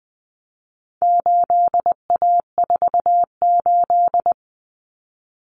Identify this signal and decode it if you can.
Morse code, heard as a keyed tone, about 20 wpm, 700 Hz. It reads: 8A48
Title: Common Cut Numbers - 4-Digit 20wpm